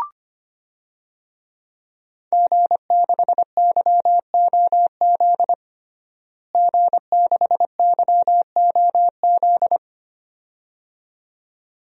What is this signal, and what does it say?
Morse code, heard as a keyed tone, about 25 wpm, 700 Hz. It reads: G6YOZ G6YOZ
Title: DXCC-100 Call Signs - 5 Characters - Repeated 2x - 25wpm